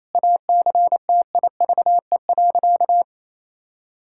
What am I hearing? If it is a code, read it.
Morse code, heard as a keyed tone, about 28 wpm, 700 Hz. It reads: ACTIVE.
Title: Sentences from Top 1000 Words - Review Part 6 - 28wpm